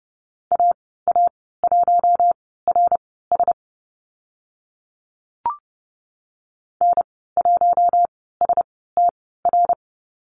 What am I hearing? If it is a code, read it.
Morse code, heard as a keyed tone, about 30 wpm, spaced as for 17 wpm, 700 Hz. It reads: AA1RS N1STR
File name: Lesson 14g - Number 2 - Call Signs 30wpm